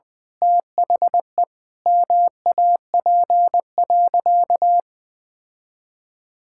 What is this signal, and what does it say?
Morse code, heard as a keyed tone, about 20 wpm, 700 Hz. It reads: THE MAP.
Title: Sentences with 5-Word Phrases - Spoken Beginning - 20wpm